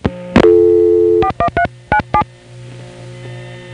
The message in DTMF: 41397